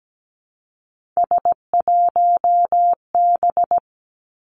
Morse code, heard as a keyed tone, about 17 wpm, 700 Hz. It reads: S1B
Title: Three Challenging Letter-Number Combinations 17wpm